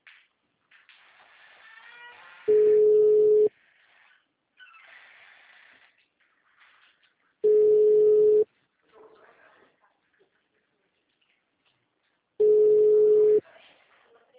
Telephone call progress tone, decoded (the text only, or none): ringback tone